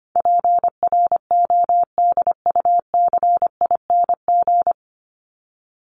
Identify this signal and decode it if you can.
Morse code, heard as a keyed tone, about 25 wpm, 700 Hz. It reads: PRODUCING
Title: ING Suffix - Top 100 Words with Base Words 25wpm